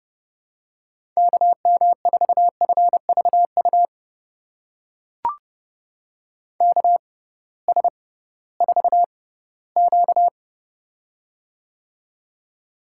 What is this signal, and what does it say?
Morse code, heard as a keyed tone, about 30 wpm, spaced as for 17 wpm, 700 Hz. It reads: KM4FVU KS4Q
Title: USA Call Signs - 6x Character Spacing - 30wpm